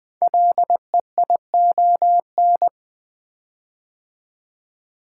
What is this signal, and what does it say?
Morse code, heard as a keyed tone, about 20 wpm, 700 Hz. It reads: LEION
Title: Code Groups - 5 Letter:Numbers - 20wpm